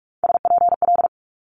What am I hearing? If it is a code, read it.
Morse code, heard as a keyed tone, about 46 wpm, 700 Hz. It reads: SPL
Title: Multi-letter Phonemes Quick Fire 45wpm